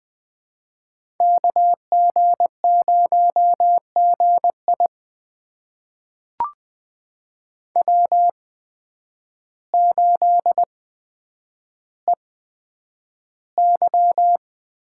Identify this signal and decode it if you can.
Morse code, heard as a keyed tone, about 20 wpm, spaced as for 9 wpm, 700 Hz. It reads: KG0GI W8EY
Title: USA Call Signs - 8x Character Spacing - 20wpm